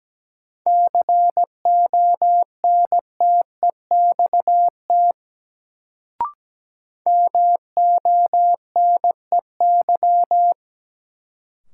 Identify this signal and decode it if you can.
Morse code, heard as a keyed tone, about 17 wpm, 700 Hz. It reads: CONTEXT MONEY
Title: Sets of 4 Words - New Word Drill - 17wpm